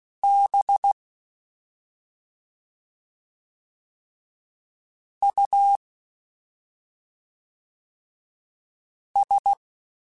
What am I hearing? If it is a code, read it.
Morse code, heard as a keyed tone, about 16 wpm, 800 Hz. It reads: B U S
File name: Special INGs / Special INGs-mp3s / SlowToFaster ing @5